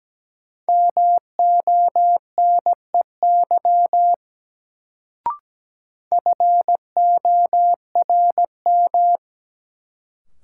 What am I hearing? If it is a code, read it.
Morse code, heard as a keyed tone, about 17 wpm, 700 Hz. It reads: MONEY FORM
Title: Sets of 4 Words - New Word Drill - 17wpm